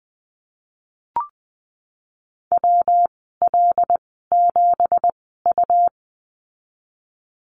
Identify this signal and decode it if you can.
Morse code, heard as a keyed tone, about 20 wpm, spaced as for 15 wpm, 700 Hz. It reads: WL7U
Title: 193 Lesson 31g - Letter v - Call Signs 20wpm